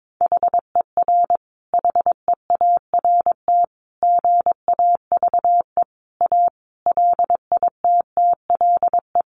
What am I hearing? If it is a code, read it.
Morse code, heard as a keyed tone, about 22 wpm, 700 Hz. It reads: HER HEART GAVE A LITTLE
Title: Sentences from Top 1300 Words 22wpm